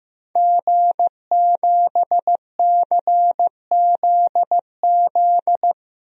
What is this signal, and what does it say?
Morse code, heard as a keyed tone, about 15 wpm, 700 Hz. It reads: G7CZZ
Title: DXCC-100 Call Signs - 5 Characters - Repeated 2x - 15wpm